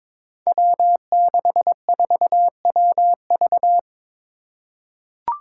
Morse code, heard as a keyed tone, about 23 wpm, 700 Hz. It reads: W64WV
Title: Five Challenging Letter-Number Combinations 22wpm